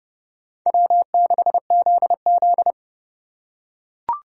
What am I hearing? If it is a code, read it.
Morse code, heard as a keyed tone, about 27 wpm, 700 Hz. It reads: W6ZZ